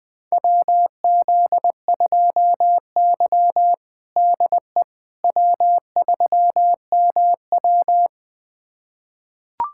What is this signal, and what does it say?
Morse code, heard as a keyed tone, about 20 wpm, 700 Hz. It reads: WZ2Y DE W3MW
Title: USA Callsign DE Callsign - 4 Characters - 20wpm